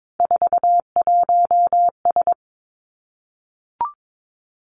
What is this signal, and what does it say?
Morse code, heard as a keyed tone, about 22 wpm, 700 Hz. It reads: 41S
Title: Three Challenging Letter-Number Combinations 22wpm